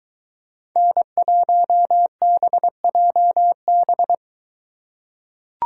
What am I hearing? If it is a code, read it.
Morse code, heard as a keyed tone, about 23 wpm, 700 Hz. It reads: N1BJB